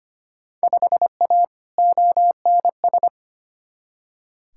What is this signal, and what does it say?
Morse code, heard as a keyed tone, about 25 wpm, 700 Hz. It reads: 5A ONS
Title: Field Day Exchanges Encore - 25wpm